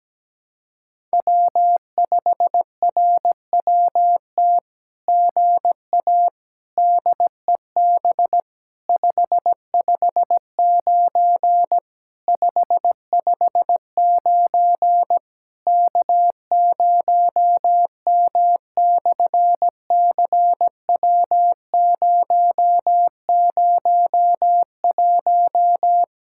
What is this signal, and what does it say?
Morse code, heard as a keyed tone, about 17 wpm, 700 Hz. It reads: W5RWT GA DEB 559 559 K0M/CW001
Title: SOTA QSO Practice 17wpm